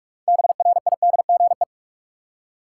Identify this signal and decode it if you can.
Morse code, heard as a keyed tone, about 46 wpm, 700 Hz. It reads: BRIDGE
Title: Six Letter Words - 45wpm